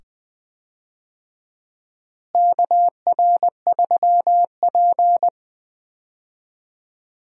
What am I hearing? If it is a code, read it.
Morse code, heard as a keyed tone, about 20 wpm, 700 Hz. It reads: KR3P
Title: USA Call Signs - 4 Characters - Repeated 2x - 20wpm